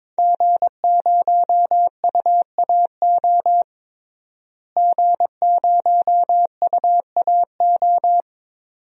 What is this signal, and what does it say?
Morse code, heard as a keyed tone, about 22 wpm, 700 Hz. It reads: G0UAO G0UAO
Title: DXCC-100 Call Signs - 5 Characters - Repeated 2x - 22wpm